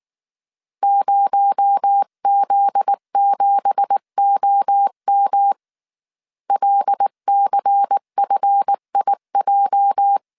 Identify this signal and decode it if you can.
Morse code, heard as a keyed tone, about 19 wpm, 800 Hz. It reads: 0Z7OM LCFIJ
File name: CW / DET / 1 9 1500C S B 18wpm 2551